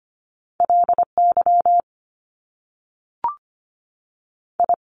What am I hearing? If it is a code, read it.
Morse code, heard as a keyed tone, about 25 wpm, 700 Hz. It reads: L Y I
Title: Top Two Letter Combinations in Words 25wpm